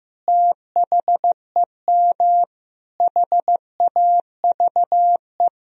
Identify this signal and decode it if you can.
Morse code, heard as a keyed tone, about 15 wpm, 700 Hz. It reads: THEM HAVE
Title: Sentences from Top 1000 Words - Review Part 3 - 15wpm